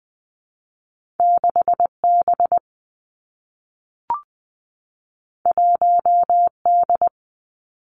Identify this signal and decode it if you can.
Morse code, heard as a keyed tone, about 19 wpm, 700 Hz. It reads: 6 B 1 D
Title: Two Challenging Letter-Number Combinations 20wpm